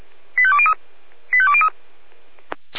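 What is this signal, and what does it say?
Five-tone selective call (ZVEI), heard as quick runs of five tones, tones of about 70 ms. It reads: 85203 85203
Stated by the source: ZVEI five-tone call 85330 received by SDR.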